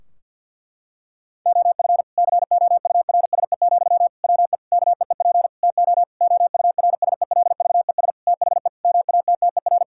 Morse code, heard as a keyed tone, about 46 wpm, 700 Hz. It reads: OF COURSE, WE KEEP TO OURSELVES THE MATTER
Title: Sentences from Top 1600 Words 50wpm